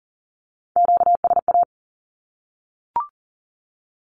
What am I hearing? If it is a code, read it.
Morse code, heard as a keyed tone, about 37 wpm, 700 Hz. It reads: QSA
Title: Q-Codes 40wpm